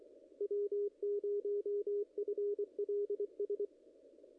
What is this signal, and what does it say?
Morse code, heard as a keyed tone, about 24 wpm, 410 Hz, with radio noise. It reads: W0FLS
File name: contests / ss-cw-k5tr-op-n5rz -0100